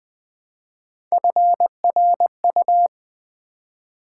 Morse code, heard as a keyed tone, about 20 wpm, 700 Hz. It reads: FRU
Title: Build A Word - 1 Letter At A Time - 5-Letter Words - 20wpm